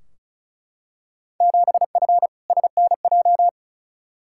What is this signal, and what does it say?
Morse code, heard as a keyed tone, about 34 wpm, 700 Hz. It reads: 7F SNJ
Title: Field Day Exchanges Encore - 35wpm